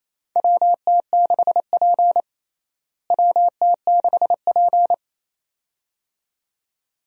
Morse code, heard as a keyed tone, about 27 wpm, 700 Hz. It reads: WT6P WT6P
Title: USA Call Signs - 4 Characters - Repeated 2x - 28wpm